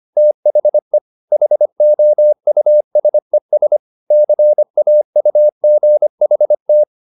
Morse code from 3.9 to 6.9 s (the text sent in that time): CAUGHT